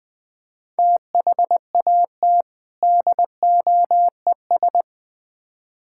Morse code, heard as a keyed tone, about 21 wpm, 700 Hz. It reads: THAT DOES
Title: Build A Phrase - 4-Word Phrases - 20wpm